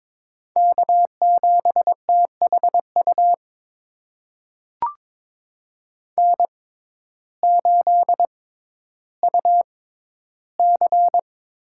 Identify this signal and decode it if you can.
Morse code, heard as a keyed tone, about 22 wpm, spaced as for 12 wpm, 700 Hz. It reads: K7THU N8UC
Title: USA Call Signs - 5 Characters - 6x Character Spacing - 22wpm